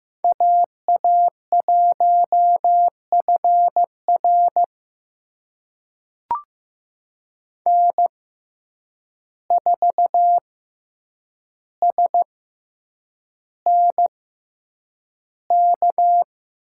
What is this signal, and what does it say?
Morse code, heard as a keyed tone, about 15 wpm, spaced as for 8 wpm, 700 Hz. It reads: AA1FR N4SNK
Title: USA Call Signs - 5 Characters - 6x Character Spacing - 15wpm